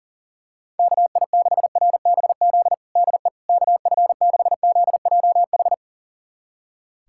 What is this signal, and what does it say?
Morse code, heard as a keyed tone, about 40 wpm, 700 Hz. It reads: KI6RBZ DE KF6ZJH
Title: USA Callsign DE Callsign - 6 Characters - 40wpm